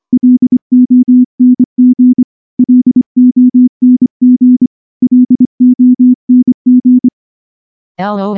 Morse code, heard as a keyed tone, about 24 wpm, 250 Hz. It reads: LONG LONG LONG
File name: morse-training-25WPM